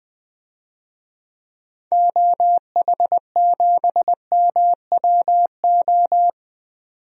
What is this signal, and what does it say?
Morse code, heard as a keyed tone, about 20 wpm, 700 Hz. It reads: OH7MWO
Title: DXCC-100 Call Signs - 6 Characters - 20wpm